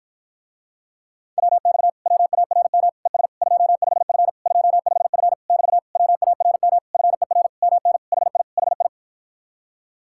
Morse code, heard as a keyed tone, about 54 wpm, 700 Hz. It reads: WX WARM ES 25F 25F <BT> WARM FER MN HI HI